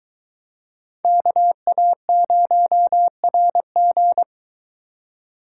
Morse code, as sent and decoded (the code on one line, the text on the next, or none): -.- .- ----- .-. --.
KA0RG